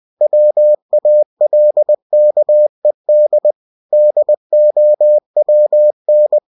Morse code, heard as a keyed tone, about 20 wpm, 600 Hz. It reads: WALKED DOWN
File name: cw-books / winnie-the-pooh / winnie-20wpm-page-0099